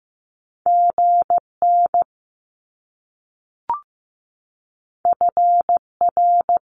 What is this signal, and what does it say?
Morse code, heard as a keyed tone, about 15 wpm, 700 Hz. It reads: GN FR
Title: Top Two Letter Combinations in Words 15wpm